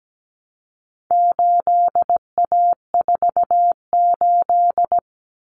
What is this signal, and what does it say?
Morse code, heard as a keyed tone, about 17 wpm, 700 Hz. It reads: 8A48
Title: Common Cut Numbers - 4-Digit 17wpm